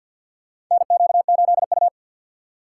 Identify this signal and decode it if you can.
Morse code, heard as a keyed tone, about 54 wpm, 700 Hz. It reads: NQ8U
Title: USA Call Signs - 4 Characters - 6x Character Spacing - 50wpm